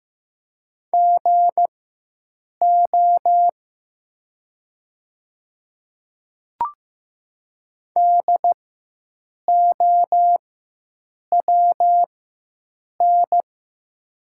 Morse code, heard as a keyed tone, about 15 wpm, spaced as for 7 wpm, 700 Hz. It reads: GO DOWN
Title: 4x Character Spacing - Top 100 Words - 15wpm